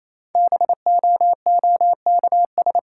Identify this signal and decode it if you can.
Morse code, heard as a keyed tone, about 28 wpm, 700 Hz. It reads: BOOKS
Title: Sentences from Top 1000 Words - Review Part 4 - New Word Drill - 28wpm